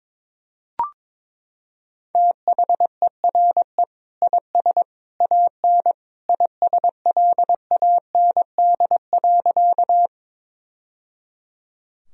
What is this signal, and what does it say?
Morse code, heard as a keyed tone, about 21 wpm, 700 Hz. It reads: THERE IS AN ISLAND.
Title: Sentences from Top 1000 Words - Review Part 3 - 22wpm